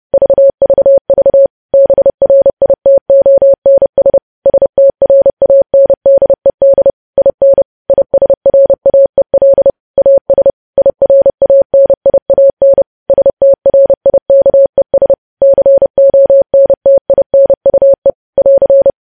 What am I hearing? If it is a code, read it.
Morse code, heard as a keyed tone, about 30 wpm, 550 Hz. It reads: VVV BRITONS STRANDED IN ISRAEL AS IRANIAN STRIKES CONTINUE <AR>
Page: news headlines in morse code at 30 wpm